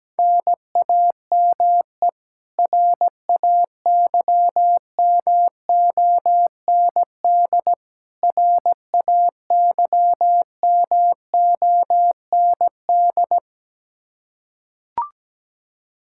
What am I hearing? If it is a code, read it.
Morse code, heard as a keyed tone, about 17 wpm, 700 Hz. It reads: NAME RAYMOND RAYMOND
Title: Practice QSO - Exchange 1 - Beginning - 17wpm